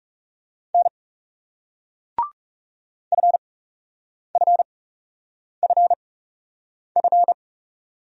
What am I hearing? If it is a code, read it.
Morse code, heard as a keyed tone, about 42 wpm, spaced as for 8 wpm, 700 Hz. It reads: N FFFF